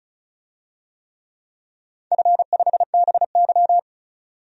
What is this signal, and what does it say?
Morse code, heard as a keyed tone, about 36 wpm, 700 Hz. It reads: F5BY
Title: DXCC-100 Call Signs - 4 Characters - 35wpm